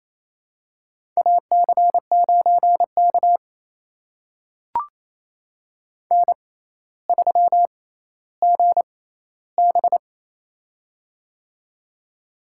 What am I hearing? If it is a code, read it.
Morse code, heard as a keyed tone, about 28 wpm, spaced as for 14 wpm, 700 Hz. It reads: AC9K N3GB
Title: USA Call Signs - 4 Characters - 6x Character Spacing - 28wpm